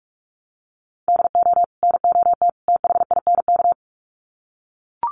Morse code, heard as a keyed tone, about 44 wpm, 700 Hz. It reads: DO NOT THINK
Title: Sets of 3 Words 45wpm